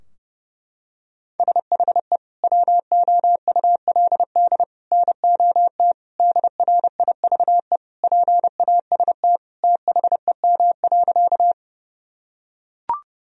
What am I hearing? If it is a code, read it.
Morse code, heard as a keyed tone, about 32 wpm, 700 Hz. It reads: SHE WOULD NOT DRIVE PAST THEM.
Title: Sentences from Top 1000 Words - Review Part 3 - 30wpm